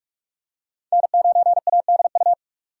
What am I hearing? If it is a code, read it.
Morse code, heard as a keyed tone, about 47 wpm, 700 Hz. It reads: N9ADU
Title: USA Call Signs - 5 Characters - 6x Character Spacing - 45wpm